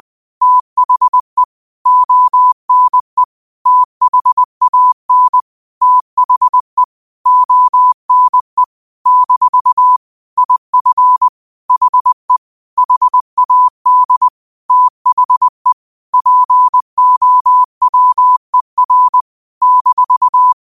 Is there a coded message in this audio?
Yes: Morse code from 0.4 to 20.5 s, heard as a keyed tone, about 20 wpm, 1000 Hz. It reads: THE ONE THAN THE ONE - IF HE HAD THE POWER -